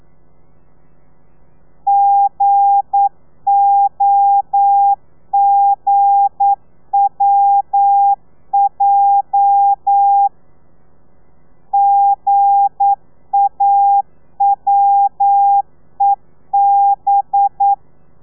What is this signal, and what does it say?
Morse code, heard as a keyed tone, about 9 wpm, 800 Hz. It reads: GOGWJ GAWEB
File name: telegrafia / i2tfj / Lez8 40